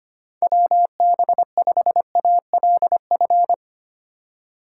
Morse code, heard as a keyed tone, about 25 wpm, 700 Hz. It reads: WB5ALF
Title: USA Call Signs - 6 Characters - 4x Character Spacing - 25wpm